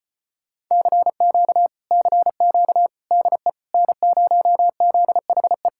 Morse code, heard as a keyed tone, about 33 wpm, 700 Hz. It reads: CQ CQ DE N0ZHE